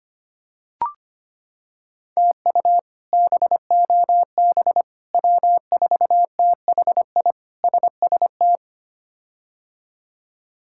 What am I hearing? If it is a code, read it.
Morse code, heard as a keyed tone, about 25 wpm, 700 Hz. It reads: TU BOB W4THI SST
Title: K1USN SST QSO Practice - 25wpm